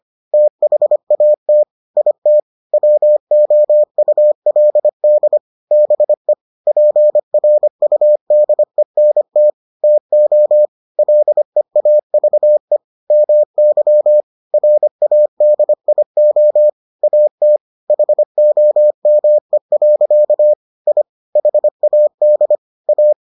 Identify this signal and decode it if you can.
Morse code, heard as a keyed tone, about 25 wpm, 600 Hz. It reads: THAT IT WOULD BE PRUDENT TO LEAVE MY RADIO AT HOME. I HAD A